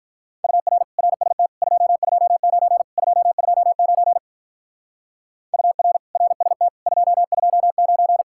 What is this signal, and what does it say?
Morse code, heard as a keyed tone, about 44 wpm, 700 Hz. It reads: UR RST 229 229 UR RST 229
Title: Speed-Racing Common QSO Phrases - 35wpm